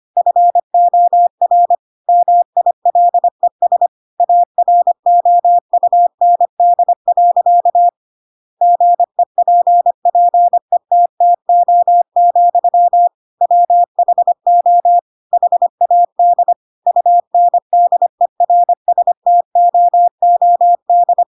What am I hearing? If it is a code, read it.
Morse code, heard as a keyed tone, about 25 wpm, 700 Hz. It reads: FOR MILES AROUND. GEPPETTO, WHO HAD UNDERSTOOD
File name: Pinocchio / Pinocchio ch7 25 25 ik1hge